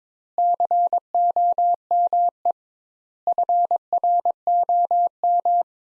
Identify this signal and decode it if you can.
Morse code, heard as a keyed tone, about 22 wpm, 700 Hz. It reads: COME FROM
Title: Sentences Top 200 Words - 22wpm